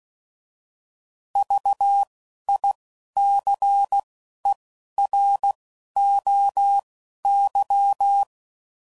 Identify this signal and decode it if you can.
Morse code, heard as a keyed tone, about 16 wpm, 800 Hz. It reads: V I C E R O Y
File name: Prefixes / By First Letter / Vs